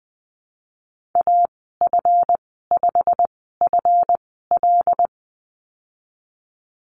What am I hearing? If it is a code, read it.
Morse code, heard as a keyed tone, about 20 wpm, spaced as for 15 wpm, 700 Hz. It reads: AF5FL
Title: 140 Lesson 23g - Letter f - Call Signs 20wpm